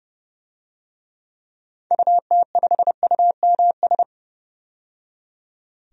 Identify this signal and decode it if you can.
Morse code, heard as a keyed tone, about 30 wpm, 700 Hz. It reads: UT5UMS
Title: DXCC-100 Call Signs - 6 Characters - 30wpm